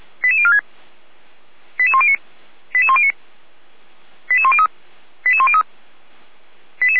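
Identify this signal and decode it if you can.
Five-tone selective call (ZVEI), heard as quick runs of five tones, tones of about 70 ms. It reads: 88046 88109 88109 88103 88103
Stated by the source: ZVEI five-tone call received on SDR, from the Feuerwehr Orlamünde (fire brigade) server audio.f85360.feuerwehr-orlamuende.org.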